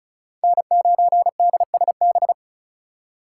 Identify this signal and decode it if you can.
Morse code, heard as a keyed tone, about 34 wpm, 700 Hz. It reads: N9DSB
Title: USA Call Signs - 5 Characters - 4x Character Spacing - 35wpm